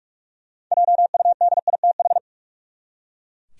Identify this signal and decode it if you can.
Morse code, heard as a keyed tone, about 43 wpm, 700 Hz. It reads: JUDITH
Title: USA Common Names - 45wpm